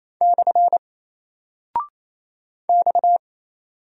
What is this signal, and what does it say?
Morse code, heard as a keyed tone, about 27 wpm, 700 Hz. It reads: / X
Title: Single Character (Letter:Number:Punctuation:Prosign) 28wpm